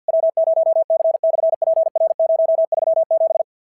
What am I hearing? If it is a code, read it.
Morse code, heard as a keyed tone, about 46 wpm, 660 Hz. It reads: W0C/PR037
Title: SOTA QSO Practice 50wpm